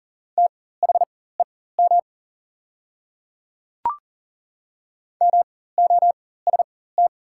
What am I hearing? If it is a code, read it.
Morse code, heard as a keyed tone, about 39 wpm, spaced as for 19 wpm, 690 Hz. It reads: THEM MOST
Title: 4x Character Spacing - Top 100 Words - 40wpm